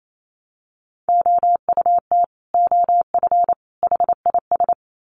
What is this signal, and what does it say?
Morse code, heard as a keyed tone, about 28 wpm, 700 Hz. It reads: OUT OF HIS
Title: Sets of 3 Words 28wpm